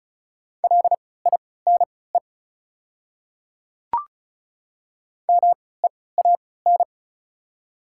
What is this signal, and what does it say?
Morse code, heard as a keyed tone, about 34 wpm, spaced as for 20 wpm, 700 Hz. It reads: LINE MEAN
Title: 3x Character Spacing - Top 200 Words - 35wpm